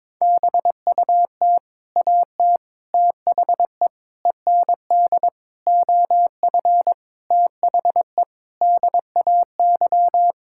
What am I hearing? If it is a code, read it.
Morse code, heard as a keyed tone, about 22 wpm, 700 Hz. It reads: BUT AT THE END OF THE DAY
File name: practice / 22wpm / Sentences with 8-Word Phrases - 22wpm